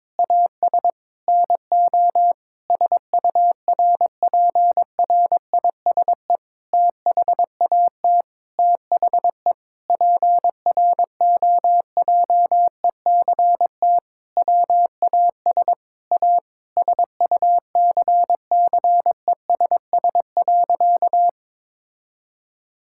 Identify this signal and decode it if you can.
Morse code, heard as a keyed tone, about 22 wpm, 700 Hz. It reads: AS NO SURPRISE THAT THE PROJECT WAS A SUCCESS.
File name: Sentences with 9-Word Phrases - 22wpm